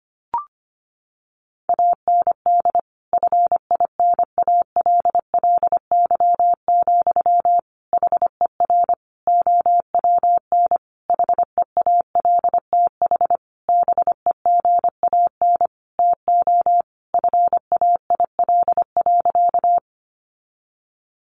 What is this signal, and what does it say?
Morse code, heard as a keyed tone, about 25 wpm, 700 Hz. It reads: AND FINALLY, HER OWN HEALTH BEGAN TO FAIL.
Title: Sentences from Top 1100 Words 25wpm